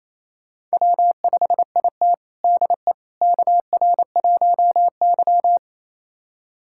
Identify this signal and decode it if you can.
Morse code, heard as a keyed tone, about 28 wpm, 700 Hz. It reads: W5IT DE KR1Y
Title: USA Callsign DE Callsign - 4 Characters - 28wpm